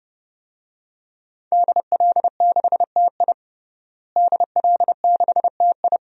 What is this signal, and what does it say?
Morse code, heard as a keyed tone, about 29 wpm, 700 Hz. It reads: DL6TI DL6TI
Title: DXCC-100 Call Signs - 5 Characters - Repeated 2x - 30wpm